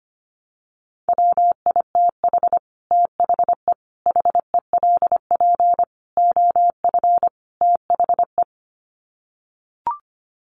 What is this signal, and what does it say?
Morse code, heard as a keyed tone, about 25 wpm, 700 Hz. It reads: WITH THE HELP OF THE
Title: Sets of 5 Words Encore 25wpm